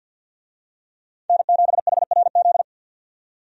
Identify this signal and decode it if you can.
Morse code, heard as a keyed tone, about 49 wpm, 700 Hz. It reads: N7HRZ
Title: USA Call Signs - 5 Characters - 50wpm